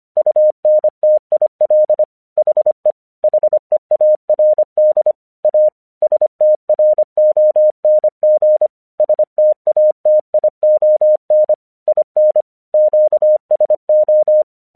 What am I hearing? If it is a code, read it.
Morse code, heard as a keyed tone, about 24 wpm, 610 Hz. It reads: UNTIL HE HEARD A STRONG STATION IN QSO
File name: 1 English 25wpm / AQC29